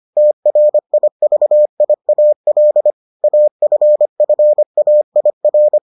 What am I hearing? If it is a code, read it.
Morse code, heard as a keyed tone, about 25 wpm, 600 Hz. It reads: TRIVIAL AFFAIR